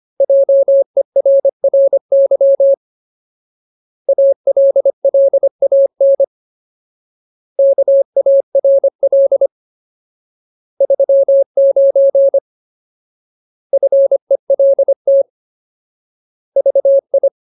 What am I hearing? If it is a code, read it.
Morse code, heard as a keyed tone, about 25 wpm, 550 Hz. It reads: JERRY ALLAN KARL 39 FELT VI
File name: Ultimate Word List / 25WPM Ultimate